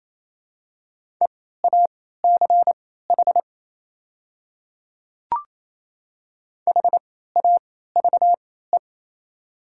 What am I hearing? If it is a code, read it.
Morse code, heard as a keyed tone, about 28 wpm, spaced as for 16 wpm, 690 Hz. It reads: EACH HAVE
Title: 3x Character Spacing - Top 100 Words - 28wpm